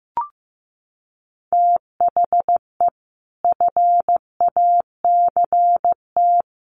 Morse code, heard as a keyed tone, about 15 wpm, 700 Hz. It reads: THE FACT
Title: Sets of 5 Words 15wpm